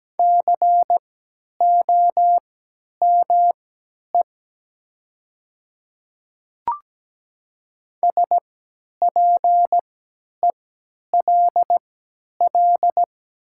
Morse code, heard as a keyed tone, about 17 wpm, spaced as for 10 wpm, 700 Hz. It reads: COME SPELL
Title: 3x Character Spacing - Top 300 Words - 17wpm